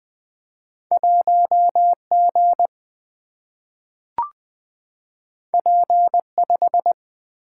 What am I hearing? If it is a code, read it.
Morse code, heard as a keyed tone, about 20 wpm, 700 Hz. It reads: 1G P5
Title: Single-Letter Single-Number 20wpm